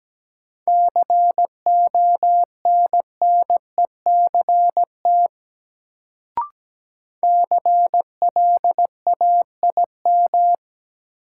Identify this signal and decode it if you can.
Morse code, heard as a keyed tone, about 17 wpm, 700 Hz. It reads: CONNECT CLAIM
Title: Sentences from Top 1500 Words - New Word Drill - 17wpm